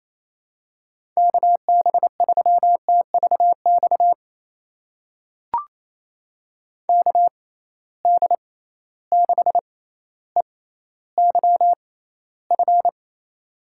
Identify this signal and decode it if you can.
Morse code, heard as a keyed tone, about 28 wpm, spaced as for 14 wpm, 700 Hz. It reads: KB3TVX KD6EYF
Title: USA Call Signs - 6x Character Spacing - 28wpm